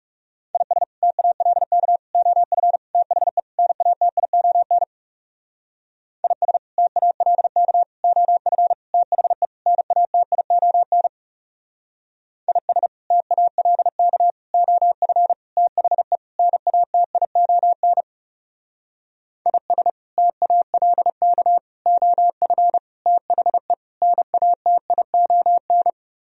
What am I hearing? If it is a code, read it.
Morse code, heard as a keyed tone, about 35 wpm, 700 Hz. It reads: IS TALK OF THE NATION IS TALK OF THE NATION IS TALK OF THE NATION IS TALK OF THE NATION
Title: Speed-Racing - Sets of 5 Words 30wpm